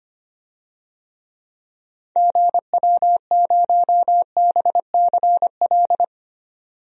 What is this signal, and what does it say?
Morse code, heard as a keyed tone, about 25 wpm, 700 Hz. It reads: GW0BCL
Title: DXCC-100 Call Signs - 6 Characters - 25wpm